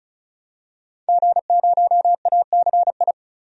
Morse code, heard as a keyed tone, about 34 wpm, 700 Hz. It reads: G0ACI
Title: DXCC-100 Call Signs - 5 Characters - 35wpm